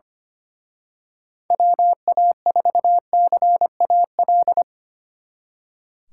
Morse code, heard as a keyed tone, about 29 wpm, 700 Hz. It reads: WA4CAL
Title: Speed Racing - USA 2x3 Call Signs 25wpm